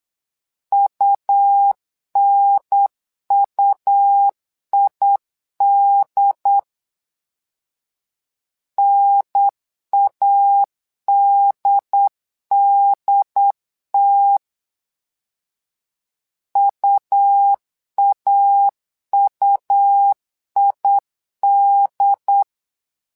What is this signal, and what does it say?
Morse code, heard as a keyed tone, about 8 wpm, 800 Hz. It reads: UNUID NADDT UAUID
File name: LEZIONI / 2 UHDN / 2 UHDN 8wpm2232